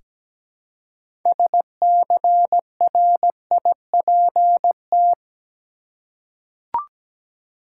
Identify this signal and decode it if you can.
Morse code, heard as a keyed tone, about 17 wpm, 700 Hz. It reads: SCRIPT
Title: Six Letter Words - 17wpm